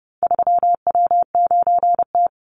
Morse code, heard as a keyed tone, about 30 wpm, 700 Hz. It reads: 3W9T
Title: Call Signs - 30wpm